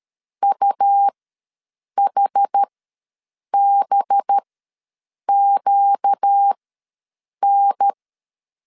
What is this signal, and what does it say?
Morse code, heard as a keyed tone, about 13 wpm, spaced as for 7 wpm, 800 Hz. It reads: UHBQN
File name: LEZIONI / 8 YJ45 / 8 YJ ws 12wpm 3342